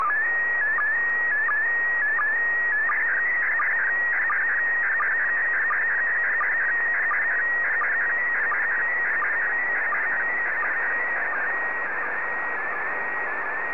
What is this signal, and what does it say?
SSTV (slow-scan television), PD 90.
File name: sstv / PD90-VK3